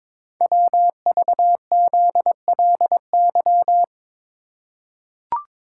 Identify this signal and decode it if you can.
Morse code, heard as a keyed tone, about 23 wpm, 700 Hz. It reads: WVZLY
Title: Code Groups - 5 Letters - 22wpm